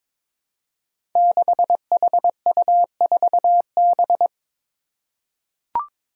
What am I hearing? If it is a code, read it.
Morse code, heard as a keyed tone, about 22 wpm, 700 Hz. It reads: 6HU4B
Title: Five Challenging Letter-Number Combinations 22wpm